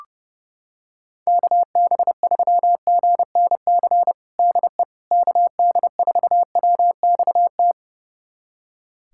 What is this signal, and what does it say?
Morse code, heard as a keyed tone, about 32 wpm, 700 Hz. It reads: KB3GNC DE KD4WXT
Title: USA Callsign DE Callsign - 6 Characters - 30wpm